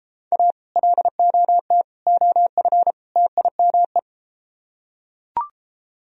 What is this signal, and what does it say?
Morse code, heard as a keyed tone, about 33 wpm, 700 Hz. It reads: A LOT OF TIME